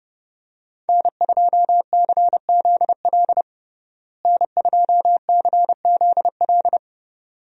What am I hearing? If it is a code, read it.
Morse code, heard as a keyed tone, about 32 wpm, 700 Hz. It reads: N2CZL N2CZL
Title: USA Call Signs - 5 Characters - Repeated 2x - 30wpm